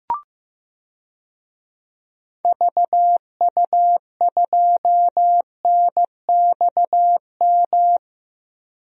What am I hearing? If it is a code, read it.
Morse code, heard as a keyed tone, about 15 wpm, 700 Hz. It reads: VU2NXM
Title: DXCC-100 Call Signs - 6 Characters - Repeated 3x - 15wpm